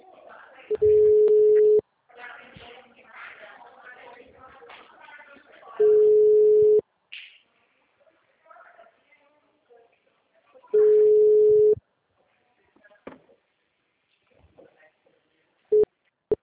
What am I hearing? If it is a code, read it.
Ringback tone (Europe, 425 Hz), with speech.
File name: out- -2009- - - .95